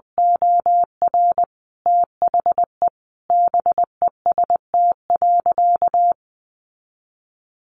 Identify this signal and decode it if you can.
Morse code, heard as a keyed tone, about 20 wpm, 700 Hz. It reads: OR THE BEST.
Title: Sentences from Top 600 Words 20wpm